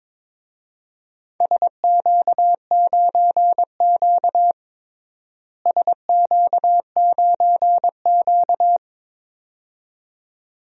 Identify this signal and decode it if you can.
Morse code, heard as a keyed tone, about 22 wpm, 700 Hz. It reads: SQ9Q SQ9Q
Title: DXCC-100 Call Signs - 4 Characters - Repeated 2x - 22wpm